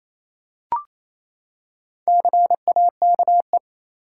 Morse code, heard as a keyed tone, about 28 wpm, 700 Hz. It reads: CAKE
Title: Four Letter Words - 28wpm